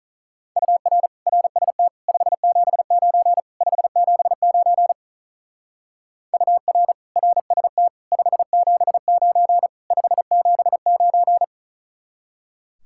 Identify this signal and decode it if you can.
Morse code, heard as a keyed tone, about 44 wpm, 700 Hz. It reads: UR RST 579 579 UR RST 579 579
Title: Speed-Racing Common QSO Phrases Encore - 35wpm